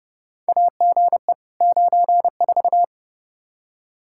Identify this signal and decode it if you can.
Morse code, heard as a keyed tone, about 30 wpm, 700 Hz. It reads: AGE 94
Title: K1USN SST QSO Practice 30wpm